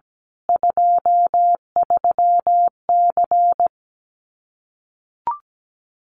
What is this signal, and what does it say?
Morse code, heard as a keyed tone, about 17 wpm, 700 Hz. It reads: 23C
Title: Common QSO Phrases 17wpm